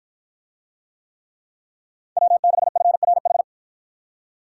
Morse code, heard as a keyed tone, about 49 wpm, 700 Hz. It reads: W6FRH